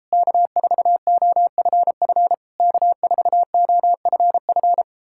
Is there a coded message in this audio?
Yes: Morse code from 0.1 to 4.8 s, heard as a keyed tone, about 38 wpm, 700 Hz. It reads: K4OFF K4OFF